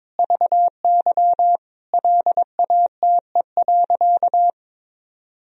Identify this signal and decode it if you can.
Morse code, heard as a keyed tone, about 22 wpm, 700 Hz. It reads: VY LATE.
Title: Sentences from Top 300 Words and Abbreviations 22wpm